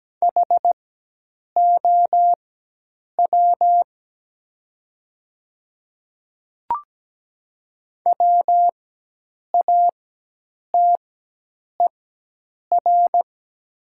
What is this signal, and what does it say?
Morse code, heard as a keyed tone, about 17 wpm, spaced as for 8 wpm, 700 Hz. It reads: HOW WATER
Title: 4x Character Spacing - Top 100 Words - 17wpm